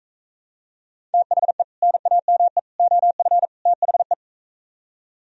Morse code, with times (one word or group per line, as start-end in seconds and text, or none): 1.1-1.6 THE
1.8-2.6 NAME
2.8-3.5 OF
3.7-4.1 THE